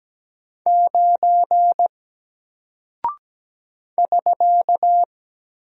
Morse code, heard as a keyed tone, about 17 wpm, 700 Hz. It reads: 9 <SK>
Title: Single Character (Letter:Number:Punctuation:Prosign) Alt 17wpm